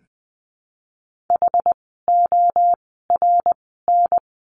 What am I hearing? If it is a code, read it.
Morse code, heard as a keyed tone, about 20 wpm, 700 Hz. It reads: H O R N
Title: 061 Lesson 11f - Letter h - Words 20wpm